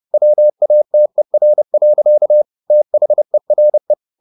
Morse code, heard as a keyed tone, about 30 wpm, 600 Hz. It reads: WATER. THERE
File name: the old man and the sea / the old man and the sea 1 30wpm